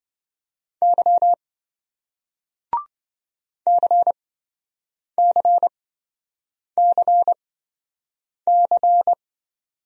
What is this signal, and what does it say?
Morse code, heard as a keyed tone, about 27 wpm, spaced as for 7 wpm, 700 Hz. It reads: Y CCCC